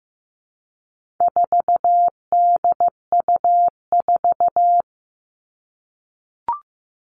Morse code, heard as a keyed tone, about 15 wpm, 700 Hz. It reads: 4DU4
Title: All Cut Numbers - 4-Digit 15wpm